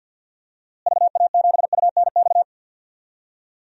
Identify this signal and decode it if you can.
Morse code, heard as a keyed tone, about 53 wpm, 700 Hz. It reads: VA7UNX
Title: DXCC-100 Call Signs - 6 Characters - Repeated 2x - 50wpm